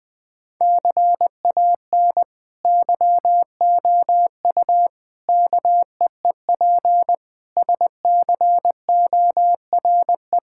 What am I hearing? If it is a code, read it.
Morse code, heard as a keyed tone, about 20 wpm, 700 Hz. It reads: CAN YOU KEEP SCORE
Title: Sentences from Top 1600 Words 20wpm